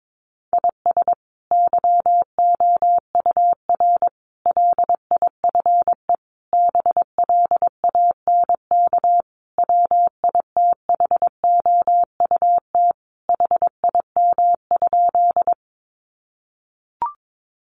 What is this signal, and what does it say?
Morse code, heard as a keyed tone, about 22 wpm, 700 Hz. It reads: IS YOUR LIFE BLANK WITHOUT HIM?
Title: Sentences from Top 1500 Words 22wpm